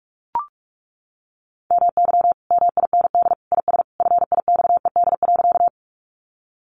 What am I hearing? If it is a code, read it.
Morse code, heard as a keyed tone, about 44 wpm, 700 Hz. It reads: MY MIND IS FIXED.
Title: Sentences from Top 1100 Words 45wpm